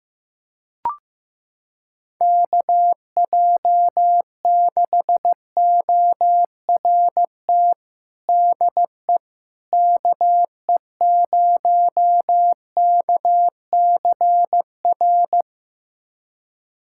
Morse code, heard as a keyed tone, about 15 wpm, 700 Hz. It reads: KJ6ORT DE KE0KCR
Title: USA Callsign DE Callsign - 6 Characters - 15wpm